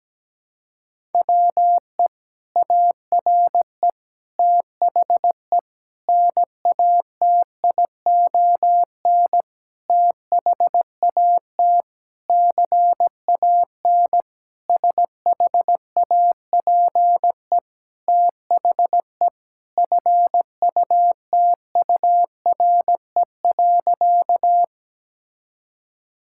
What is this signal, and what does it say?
Morse code, heard as a keyed tone, about 17 wpm, 700 Hz. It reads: WE ARE THE NATION THAT CAN SHAPE THE FUTURE.
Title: Sentences from Top 1000 Words - Review Part 6 - 17wpm